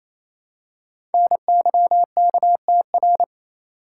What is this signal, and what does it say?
Morse code, heard as a keyed tone, about 27 wpm, 700 Hz. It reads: NYKTR
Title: Code Groups - 5 Letters - 28wpm